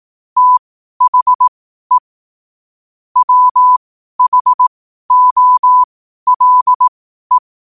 Morse code, heard as a keyed tone, about 18 wpm, spaced as for 12 wpm, 1000 Hz. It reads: THE WHOLE
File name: qotdfiles / 12wpm